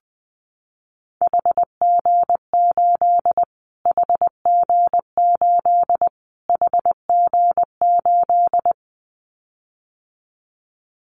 Morse code, heard as a keyed tone, about 20 wpm, 700 Hz. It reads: HG8 HG8 HG8
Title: Callsign Prefixes for Top 25 DXCC - Repeated 3x - 20wpm